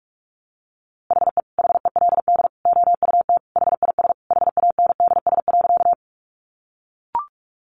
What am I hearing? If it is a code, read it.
Morse code, heard as a keyed tone, about 44 wpm, 700 Hz. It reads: HE HELD OUT HIS HANDS.
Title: A Sound Of Thunder 45wpm